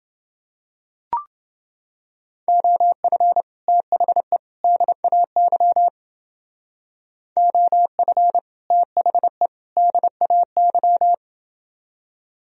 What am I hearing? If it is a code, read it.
Morse code, heard as a keyed tone, about 28 wpm, 700 Hz. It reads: OF THE DAY OF THE DAY